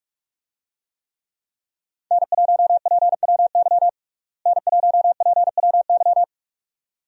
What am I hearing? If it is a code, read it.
Morse code, heard as a keyed tone, about 46 wpm, 700 Hz. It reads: N1PWY N1PWY
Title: USA Call Signs - 5 Characters - Repeated 2x - 45wpm